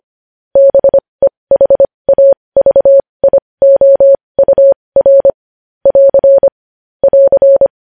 Morse code, heard as a keyed tone, about 25 wpm, spaced as for 20 wpm, 550 Hz. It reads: BEHAVIOUR <AR> <AR>
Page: news-headlines-in-morse-code-at-20-wpm